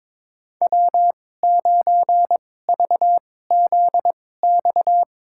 Morse code, heard as a keyed tone, about 21 wpm, spaced as for 16 wpm, 700 Hz. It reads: W9VZX
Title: USA Call Signs - 5 Characters - 2x Character Spacing - 22wpm